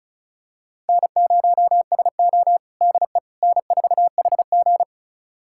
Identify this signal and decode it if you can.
Morse code, heard as a keyed tone, about 36 wpm, 700 Hz. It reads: N0SO DE N4HG
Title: USA Callsign DE Callsign - 4 Characters - 35wpm